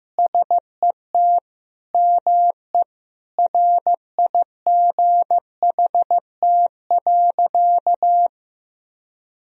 Morse code, heard as a keyed tone, about 15 wpm, 700 Hz. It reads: SET ME RIGHT.
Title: Sentences from Top 1000 Words - Review Part 1 - 15wpm